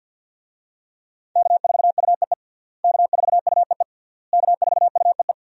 Morse code, heard as a keyed tone, about 52 wpm, 700 Hz. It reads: K4UEE K4UEE K4UEE
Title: USA Call Signs - 5 Characters - Repeated 3x - 50wpm